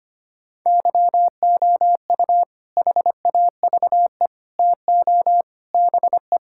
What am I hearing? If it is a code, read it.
Morse code, heard as a keyed tone, about 25 wpm, 700 Hz. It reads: YOU HAVE TO BE
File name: practice / 25wpm / Sentences with 4-Word Phrases - 25wpm